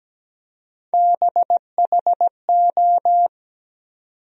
Morse code, heard as a keyed tone, about 17 wpm, 700 Hz. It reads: BHO
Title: USA Call Sign Postfixes - 3 Letters 17wpm